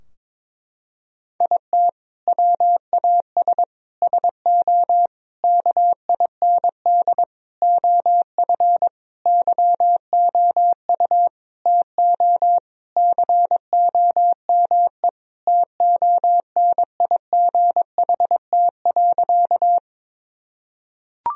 Morse code, heard as a keyed tone, about 22 wpm, 700 Hz. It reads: IT WAS SO KIND OF YOU TO COME TONIGHT.
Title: Sentences from Top 1000 Words - Review Part 3 - 22wpm